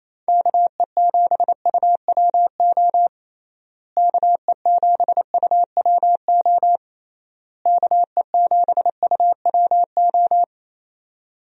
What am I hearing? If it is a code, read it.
Morse code, heard as a keyed tone, about 27 wpm, 700 Hz. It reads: KE7UWO KE7UWO KE7UWO
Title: USA Call Signs - 6 Characters - Repeated 3x - 28wpm